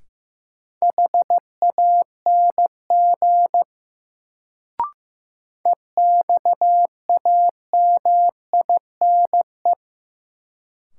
Morse code, heard as a keyed tone, about 15 wpm, 700 Hz. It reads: HANG EXAMINE
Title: Sentences from Top 1600 Words - New Word Drill - 15wpm